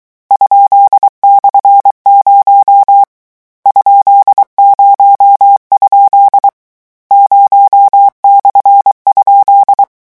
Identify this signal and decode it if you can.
Morse code, heard as a keyed tone, about 23 wpm, 800 Hz. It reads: ?/0 ?0? 0/?
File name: morse mp3 / lecon13-vitesse 22